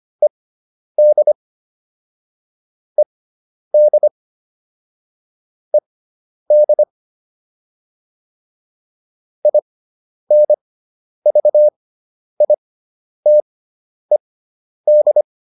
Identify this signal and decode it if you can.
Morse code, heard as a keyed tone, about 25 wpm, spaced as for 10 wpm, 600 Hz. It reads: ED ED ED INVITED